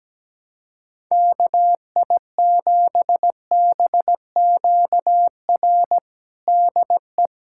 Morse code, heard as a keyed tone, about 17 wpm, 700 Hz. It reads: KI7BQR DE
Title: USA Callsign DE Callsign 17wpm